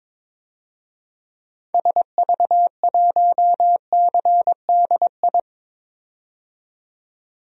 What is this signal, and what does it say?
Morse code, heard as a keyed tone, about 22 wpm, 700 Hz. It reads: SV1CDI
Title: DXCC-100 Call Signs - 6 Characters - 22wpm